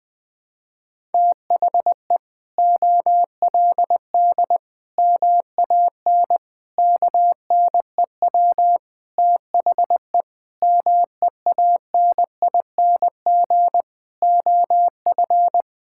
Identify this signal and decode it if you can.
Morse code, heard as a keyed tone, about 20 wpm, 700 Hz. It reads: THE OLD MAN KNEW THE MEANING OF
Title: Sentences from Top 1000 Words - Review Part 6 - 20wpm